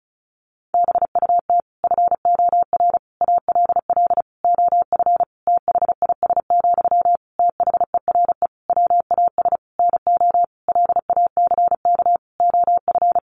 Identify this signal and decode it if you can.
Morse code, heard as a keyed tone, about 34 wpm, 700 Hz. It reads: BUT FOR ALL OF THIS, THERE WAS NO LACK OF
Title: Sentences from Top 1300 Words 35wpm